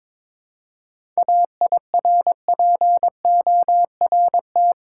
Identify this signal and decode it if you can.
Morse code, heard as a keyed tone, about 21 wpm, 700 Hz. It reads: AIRPORT
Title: Seven Letter Words - 22wpm